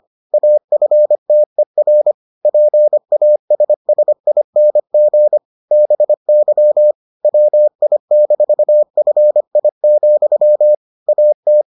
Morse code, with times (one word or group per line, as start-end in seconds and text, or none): 0.3-2.1 AFTER
2.5-5.4 PASSING
5.7-6.9 BY
7.2-10.7 WI-FI,
11.1-11.6 AT